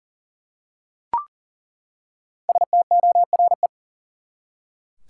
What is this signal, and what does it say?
Morse code, heard as a keyed tone, about 38 wpm, 700 Hz. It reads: STORE
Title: Five Letter Words - 40wpm